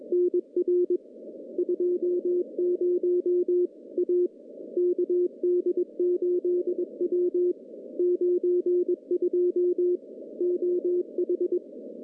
Morse code, heard as a keyed tone, about 21 wpm, 360 Hz. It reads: NR 20 A KD8W 92 OH
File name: contests / CW-SS-k5tr- -1400